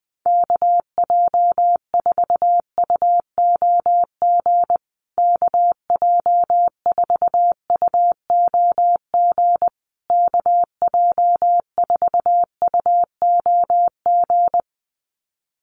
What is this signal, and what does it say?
Morse code, heard as a keyed tone, about 20 wpm, 700 Hz. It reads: KJ4UOG KJ4UOG KJ4UOG
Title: USA Call Signs Repeated 3x 20wpm